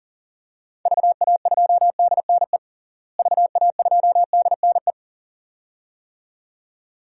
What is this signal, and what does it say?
Morse code, heard as a keyed tone, about 40 wpm, 700 Hz. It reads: VA2DNE VA2DNE
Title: DXCC-100 Call Signs - 6 Characters - Repeated 2x - 40wpm